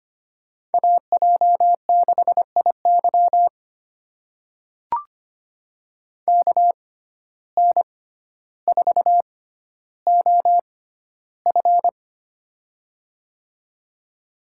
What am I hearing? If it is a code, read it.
Morse code, heard as a keyed tone, about 25 wpm, spaced as for 13 wpm, 700 Hz. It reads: AJ6IY KN4OF
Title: USA Call Signs - 5 Characters - 6x Character Spacing - 25wpm